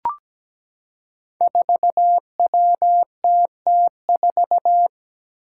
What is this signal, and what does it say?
Morse code, heard as a keyed tone, about 17 wpm, 700 Hz. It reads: 4WTT4
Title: Five Challenging Letter-Number Combinations 17wpm